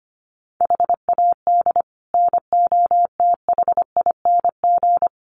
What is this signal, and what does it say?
Morse code, heard as a keyed tone, about 25 wpm, 700 Hz. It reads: HAD NOTHING
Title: Sets of 4 Words 25wpm